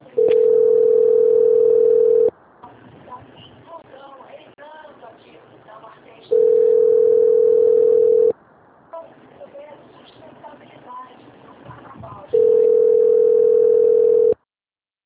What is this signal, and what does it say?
Ringback tone (North America), 440 + 480 Hz, with speech.
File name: out- - - - - .71020